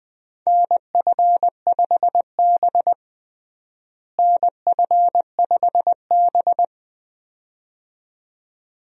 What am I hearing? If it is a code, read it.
Morse code, heard as a keyed tone, about 20 wpm, 700 Hz. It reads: NF5B NF5B
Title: USA Call Signs - 4 Characters - Repeated 2x - 20wpm